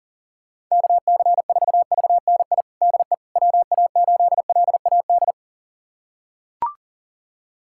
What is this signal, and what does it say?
Morse code, heard as a keyed tone, about 40 wpm, 700 Hz. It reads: KC4VNI DE WA8LAD
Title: USA Callsign DE Callsign 40wpm